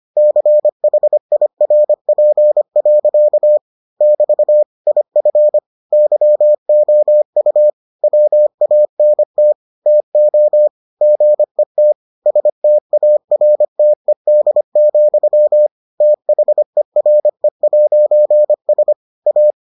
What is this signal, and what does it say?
Morse code, heard as a keyed tone, about 25 wpm, 600 Hz. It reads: CHIRP. <BT> IF YOU WANT TO GET STARTED, THERE'S A